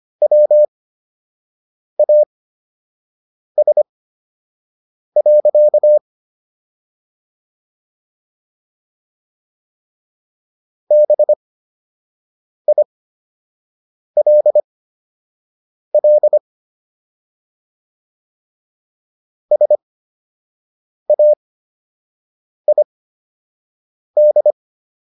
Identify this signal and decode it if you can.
Morse code, heard as a keyed tone, about 25 wpm, spaced as for 6 wpm, 600 Hz. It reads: WAS. BILL SAID